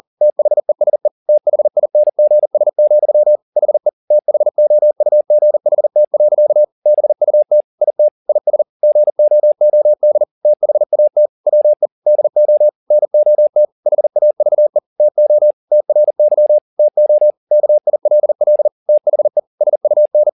Morse code, heard as a keyed tone, about 40 wpm, 600 Hz. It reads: THESE THINGS, HE THOUGHT. BUT IT IS GOOD THAT WE DO NOT HAVE TO TRY TO KILL THE SUN
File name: the old man and the sea 2 40wpm